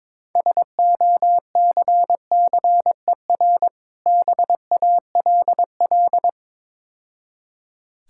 Morse code, heard as a keyed tone, about 23 wpm, 700 Hz. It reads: SOCCER BALL
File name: Easy Words - Speed-Racing with Hints - 22wpm